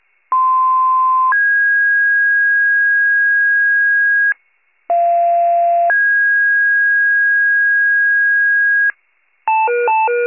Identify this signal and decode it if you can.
Two-tone page, heard as a one-second tone, then a three-second tone, about 1040 then 1650 Hz, about 690 then 1650 Hz. It is heard from 0.3 to 8.9 s.